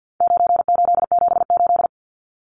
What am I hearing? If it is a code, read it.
Morse code, heard as a keyed tone, about 50 wpm, 700 Hz. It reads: 9878
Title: Numbers - 4-Digit 50wpm